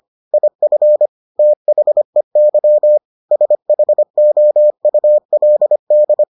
Morse code, heard as a keyed tone, about 25 wpm, 600 Hz. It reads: IF THEY SHOULD